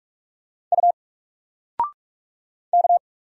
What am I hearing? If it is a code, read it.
Morse code, heard as a keyed tone, about 46 wpm, 700 Hz. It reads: U K
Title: Single Letter-Number 45wpm